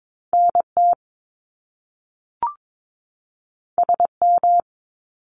Morse code, heard as a keyed tone, about 22 wpm, 700 Hz. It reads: NT SM
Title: Top Two Letter Combinations in Words 22wpm